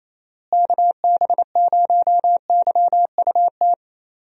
Morse code, heard as a keyed tone, about 28 wpm, 700 Hz. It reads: KB0YUT
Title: USA Call Signs - 6 Characters - Repeated 2x - 28wpm